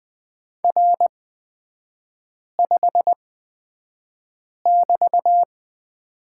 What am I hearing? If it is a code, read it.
Morse code, heard as a keyed tone, about 20 wpm, spaced as for 5 wpm, 700 Hz. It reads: R5<BT>
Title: 243 Lesson 39e - Letter z - Three Characters 20wpm